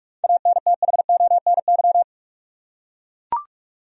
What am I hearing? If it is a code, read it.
Morse code, heard as a keyed tone, about 42 wpm, 700 Hz. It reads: ANTHONY
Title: USA Common Names - 45wpm